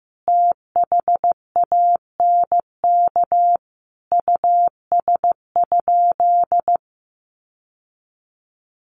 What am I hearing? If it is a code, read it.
Morse code, heard as a keyed tone, about 15 wpm, 700 Hz. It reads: THANK US?
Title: Sentences from Top 600 Words 15wpm